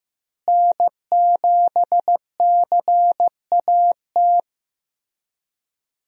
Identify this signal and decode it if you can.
Morse code, heard as a keyed tone, about 15 wpm, 700 Hz. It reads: N7CAT
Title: USA Call Signs - 5 Characters - Repeated 3x - 15wpm